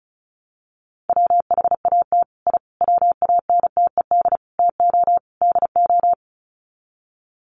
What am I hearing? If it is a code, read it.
Morse code, heard as a keyed tone, about 34 wpm, 700 Hz. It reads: WHAT I WANTED TO DO
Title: Sets of 5 Words Encore2 35wpm